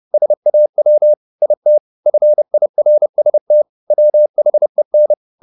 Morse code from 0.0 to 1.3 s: SAW

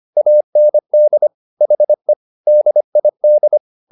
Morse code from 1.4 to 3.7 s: HE DID

